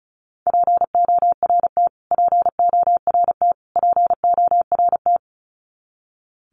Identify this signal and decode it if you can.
Morse code, heard as a keyed tone, about 34 wpm, 700 Hz. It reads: PORT PORT PORT
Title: Top 500 Words Repeated 35wpm